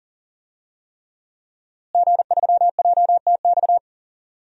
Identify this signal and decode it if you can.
Morse code, heard as a keyed tone, about 40 wpm, 700 Hz. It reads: G3JTX
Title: DXCC-100 Call Signs - 5 Characters - Repeated 3x - 40wpm